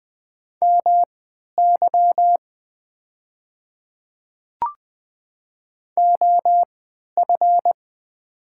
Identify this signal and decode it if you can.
Morse code, heard as a keyed tone, about 20 wpm, spaced as for 11 wpm, 700 Hz. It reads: MY OF
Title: 3x Character Spacing - Top 100 Words - 20wpm